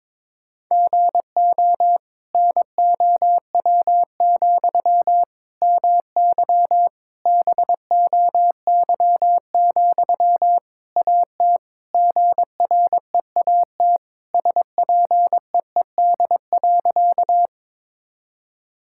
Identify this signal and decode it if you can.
Morse code, heard as a keyed tone, about 22 wpm, 700 Hz. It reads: GO NOW, MY BOY, AT GREAT SPEED.
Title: Sentences from Top 1000 Words - Review Part 5 - 22wpm